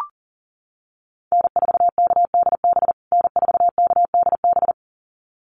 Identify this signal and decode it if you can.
Morse code, heard as a keyed tone, about 37 wpm, 700 Hz. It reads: N4KDB N4KDB
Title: USA Call Signs Repeated 2x 40wpm